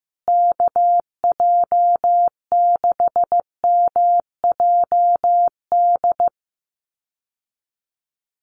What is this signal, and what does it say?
Morse code, heard as a keyed tone, about 15 wpm, 700 Hz. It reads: KJ6MJD
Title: USA Call Signs Repeated 3x 15wpm